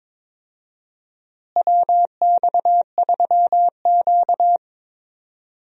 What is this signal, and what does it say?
Morse code, heard as a keyed tone, about 22 wpm, 700 Hz. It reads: WX3Q
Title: USA Call Signs - 4 Characters - Repeated 2x - 22wpm